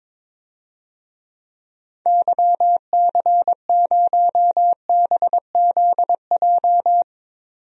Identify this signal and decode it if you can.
Morse code, heard as a keyed tone, about 22 wpm, 700 Hz. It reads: YC0BZJ
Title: DXCC-100 Call Signs - 6 Characters - 22wpm